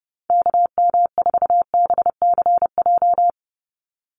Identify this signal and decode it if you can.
Morse code, heard as a keyed tone, about 30 wpm, 700 Hz. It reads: KM4BCJ
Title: USA Call Signs 30wpm